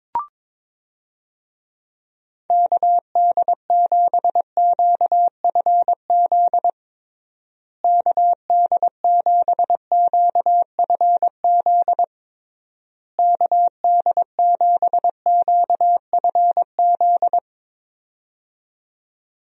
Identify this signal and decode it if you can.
Morse code, heard as a keyed tone, about 22 wpm, 700 Hz. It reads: KD7QFZ KD7QFZ KD7QFZ
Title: USA Call Signs - 6 Characters - Repeated 3x - 22wpm